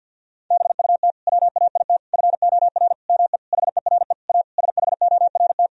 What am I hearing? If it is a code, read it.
Morse code, heard as a keyed tone, about 50 wpm, 700 Hz. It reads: BUT WAIT FOR ME HERE A SHORT
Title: Sentences from Top 1000 Words - Review Part 2 - 50wpm